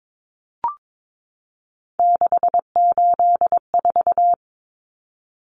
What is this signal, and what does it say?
Morse code, heard as a keyed tone, about 22 wpm, 700 Hz. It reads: 684
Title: Numbers - 3-Digit 22wpm